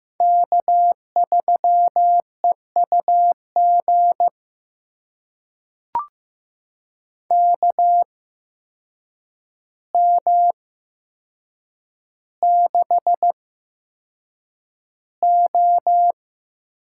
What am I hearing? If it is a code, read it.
Morse code, heard as a keyed tone, about 15 wpm, spaced as for 7 wpm, 700 Hz. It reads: K3EUG KM6O
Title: USA Call Signs - 8x Character Spacing - 15wpm